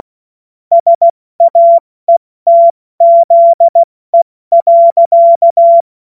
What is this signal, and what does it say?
Morse code, heard as a keyed tone, about 16 wpm, 680 Hz. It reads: SAETZE.
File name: Morsekurs / Witze / Witz 10. August 2021, 70 F80